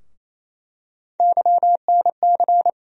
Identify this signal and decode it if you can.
Morse code, heard as a keyed tone, about 28 wpm, 700 Hz. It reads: YNC
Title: USA Call Sign Postfixes - 3 Letters 28wpm